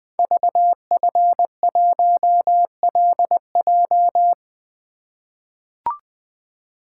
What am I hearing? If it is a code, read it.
Morse code, heard as a keyed tone, about 20 wpm, 700 Hz. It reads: VF1LJ
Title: Code Groups - 5 Letter:Numbers - 20wpm